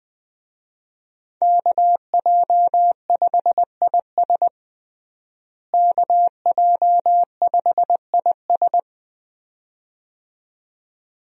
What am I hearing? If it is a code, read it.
Morse code, heard as a keyed tone, about 20 wpm, 700 Hz. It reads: KJ5IS KJ5IS
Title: USA Call Signs - 5 Characters - Repeated 2x - 20wpm